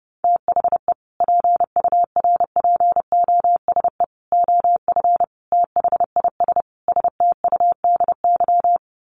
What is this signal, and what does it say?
Morse code, heard as a keyed tone, about 28 wpm, 700 Hz. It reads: THE PURPOSE OF THIS STUDY
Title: Sets of 5 Words 30wpm